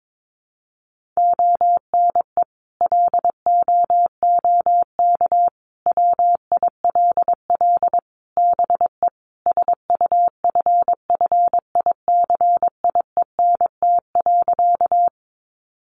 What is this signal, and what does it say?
Morse code, heard as a keyed tone, about 22 wpm, 700 Hz. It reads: ONE LOOK WILL BE SUFFICIENT.
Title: Sentences from Top 1400 Words 22wpm